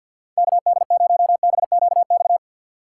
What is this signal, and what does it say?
Morse code, heard as a keyed tone, about 53 wpm, 700 Hz. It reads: KD0BQX
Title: USA Call Signs - 6 Characters - 8x Character Spacing - 50wpm